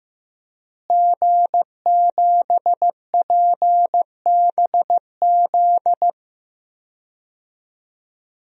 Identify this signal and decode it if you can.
Morse code, heard as a keyed tone, about 15 wpm, 700 Hz. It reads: G7PBZ
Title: DXCC-100 Call Signs - 5 Characters - Repeated 3x - 15wpm